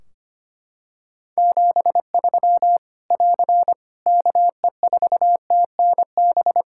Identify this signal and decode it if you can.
Morse code, heard as a keyed tone, about 25 wpm, 700 Hz. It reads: 73 <AR> KE4TNB
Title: Practice QSO - Exchange 4 - Ending - 25wpm